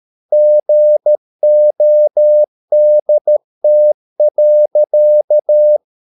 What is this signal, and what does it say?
Morse code, heard as a keyed tone, about 13 wpm, 600 Hz. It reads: GODT.
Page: Ferie 13 wpm